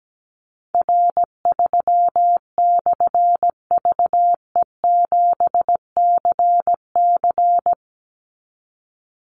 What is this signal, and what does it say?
Morse code, heard as a keyed tone, about 17 wpm, 700 Hz. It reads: R3/VE7CC
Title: Call Signs - 17wpm